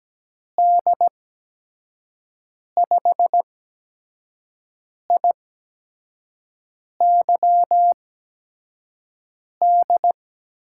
Morse code, heard as a keyed tone, about 17 wpm, spaced as for 5 wpm, 700 Hz. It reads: D5IYD
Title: USA Call Signs - 6 Characters - 8x Character Spacing - 17wpm